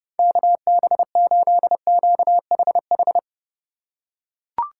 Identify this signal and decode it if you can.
Morse code, heard as a keyed tone, about 30 wpm, 700 Hz. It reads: KB8QHH